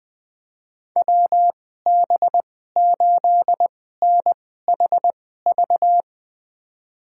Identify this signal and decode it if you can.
Morse code, heard as a keyed tone, about 20 wpm, spaced as for 15 wpm, 700 Hz. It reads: WB8NHV
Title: USA Call Signs - 6 Characters - 2x Character Spacing - 20wpm